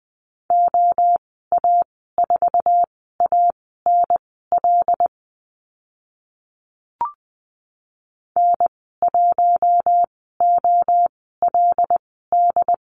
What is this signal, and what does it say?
Morse code, heard as a keyed tone, about 20 wpm, spaced as for 14 wpm, 700 Hz. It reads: OA4ANL N1OLD
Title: 082 Lesson 14g - Number 2 - Call Signs 20wpm